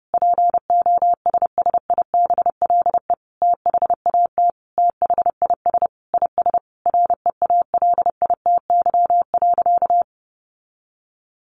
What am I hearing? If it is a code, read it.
Morse code, heard as a keyed tone, about 28 wpm, 700 Hz. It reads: POSSIBLE THAT THIS IS REALITY.
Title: Sentences from Top 1200 Words 30wpm